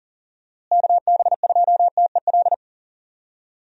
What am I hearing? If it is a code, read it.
Morse code, heard as a keyed tone, about 40 wpm, 700 Hz. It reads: KB2TEL
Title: USA Call Signs - 6 Characters - Repeated 3x - 40wpm